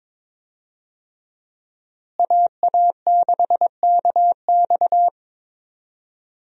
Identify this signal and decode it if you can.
Morse code, heard as a keyed tone, about 22 wpm, 700 Hz. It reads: AA6KX
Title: USA Call Signs - 5 Characters - Repeated 2x - 22wpm